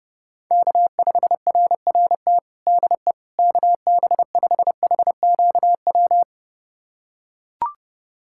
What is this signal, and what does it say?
Morse code, heard as a keyed tone, about 32 wpm, 690 Hz. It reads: K5RRT DE KB5HQW
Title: USA Callsign DE Callsign 30wpm